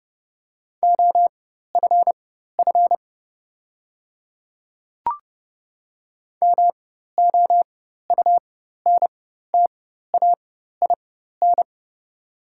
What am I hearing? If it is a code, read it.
Morse code, heard as a keyed tone, about 30 wpm, spaced as for 14 wpm, 700 Hz. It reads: OFF MOUNTAIN
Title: 4x Character Spacing - Top 300 Words - 30wpm